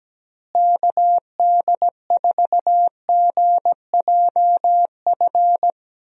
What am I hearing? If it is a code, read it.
Morse code, heard as a keyed tone, about 17 wpm, 700 Hz. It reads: KD4GJF
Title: USA Call Signs - 6 Characters - Repeated 3x - 17wpm